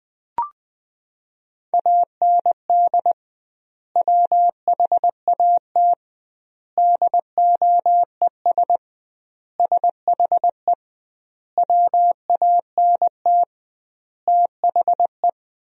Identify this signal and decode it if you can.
Morse code, heard as a keyed tone, about 21 wpm, 700 Hz. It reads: AND WHAT DOES SHE WANT THE
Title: 2x Word-Spacing - Sentences from Top 400 Words - 20wpm